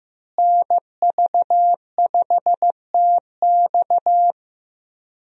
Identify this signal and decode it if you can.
Morse code, heard as a keyed tone, about 15 wpm, 700 Hz. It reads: NV5TX
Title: USA Call Signs - 5 Characters - Repeated 2x - 15wpm